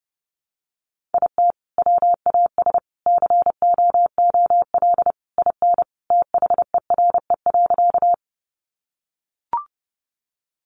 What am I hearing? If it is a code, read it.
Morse code, heard as a keyed tone, about 28 wpm, 700 Hz. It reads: IT WAS COOL IN THERE.
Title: Sentences from Top 1200 Words 30wpm